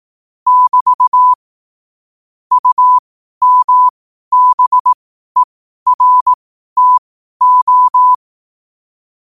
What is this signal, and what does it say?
Morse code, heard as a keyed tone, about 18 wpm, spaced as for 12 wpm, 1000 Hz. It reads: <BT> UMBERTO
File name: qotdfiles / 12wpm